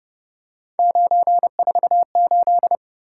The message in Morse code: 948

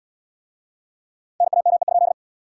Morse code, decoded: NAME?

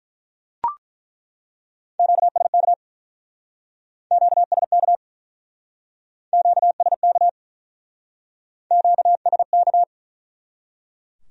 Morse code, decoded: QSK QSK QSK QSK

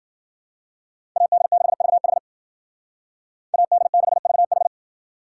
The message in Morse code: AD6VH AD6VH

